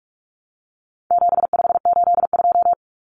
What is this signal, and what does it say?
Morse code, heard as a keyed tone, about 44 wpm, 700 Hz. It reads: 7582
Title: Numbers - 4-Digit 45wpm